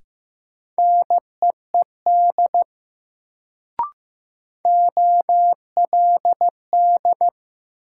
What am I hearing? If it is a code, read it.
Morse code, heard as a keyed tone, about 15 wpm, 700 Hz. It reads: NEED OLD
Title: Sets of 2 Words - New Word Drill - 15wpm